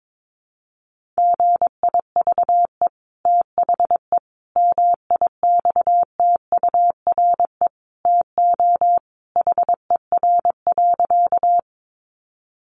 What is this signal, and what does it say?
Morse code, heard as a keyed tone, about 22 wpm, 700 Hz. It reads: GIVE THE MIXTURE TO HER.
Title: Sentences from Top 1500 Words 22wpm